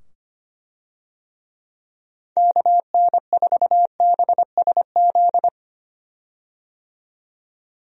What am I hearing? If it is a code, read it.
Morse code, heard as a keyed tone, about 25 wpm, 700 Hz. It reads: KN4BSZ
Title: USA Call Signs - 6 Characters - 25wpm